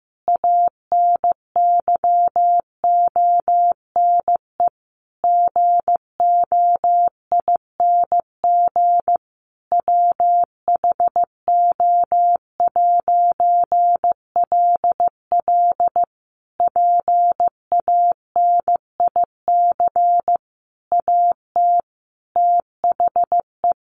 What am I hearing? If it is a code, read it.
Morse code, heard as a keyed tone, about 15 wpm, 700 Hz. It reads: ANYONE GOING WHO'LL PANIC AT THE
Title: A Sound Of Thunder 15wpm